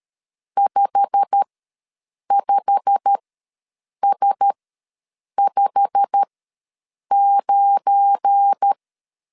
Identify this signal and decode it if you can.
Morse code, heard as a keyed tone, about 13 wpm, spaced as for 7 wpm, 800 Hz. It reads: 55S59
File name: LEZIONI / 8 YJ45 / 8 YJ45 ws 12wpm 3146